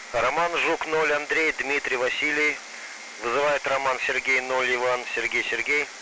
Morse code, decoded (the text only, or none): none